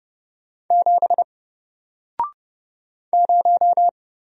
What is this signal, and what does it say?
Morse code, heard as a keyed tone, about 30 wpm, 700 Hz. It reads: 7 0
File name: Single-Letter-Number-30wpm